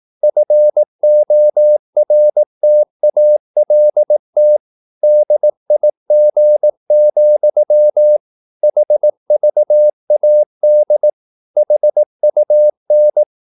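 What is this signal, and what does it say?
Morse code, heard as a keyed tone, about 18 wpm, 600 Hz. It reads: FORTALT DIG, HVAD HUN
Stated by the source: Slow (QRS) Morse practice at 18 wpm, CW med Gnister.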